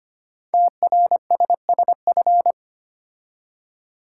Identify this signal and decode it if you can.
Morse code, heard as a keyed tone, about 25 wpm, 700 Hz. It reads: TRSSF
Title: Code Groups - 5 Letters - 25wpm